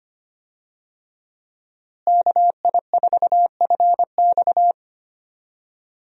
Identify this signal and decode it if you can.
Morse code, heard as a keyed tone, about 25 wpm, 700 Hz. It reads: KI4FX
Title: USA Call Signs - 5 Characters - Repeated 3x - 25wpm